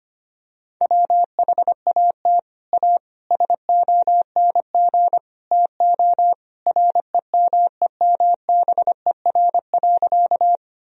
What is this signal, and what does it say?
Morse code, heard as a keyed tone, about 25 wpm, 700 Hz. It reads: WHAT A SONG TO REMEMBER.
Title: Sentences from Top 1000 Words - Review Part 2 - 25wpm